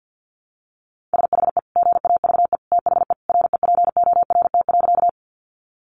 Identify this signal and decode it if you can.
Morse code, heard as a keyed tone, about 48 wpm, 700 Hz. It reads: SHE GAVE THE REPORT.
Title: Sentences from Top 900 Words 50wpm